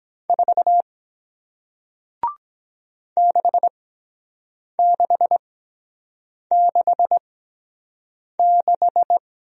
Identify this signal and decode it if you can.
Morse code, heard as a keyed tone, about 23 wpm, spaced as for 7 wpm, 700 Hz. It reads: 4 6666